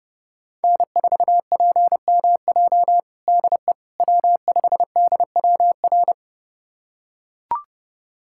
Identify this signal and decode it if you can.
Morse code, heard as a keyed tone, about 32 wpm, 700 Hz. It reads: N4PMJ DE W5DWR
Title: USA Callsign DE Callsign - 5 Characters - 30wpm